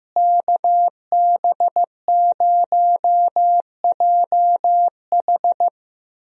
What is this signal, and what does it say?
Morse code, heard as a keyed tone, about 16 wpm, 700 Hz. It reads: KB0JH
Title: USA Call Signs - 5 Characters - Repeated 3x - 15wpm